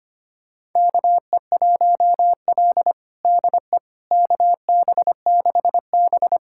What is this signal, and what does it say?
Morse code, heard as a keyed tone, about 25 wpm, 700 Hz. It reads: KE1L DE KB6B
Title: USA Callsign DE Callsign - 4 Characters - 25wpm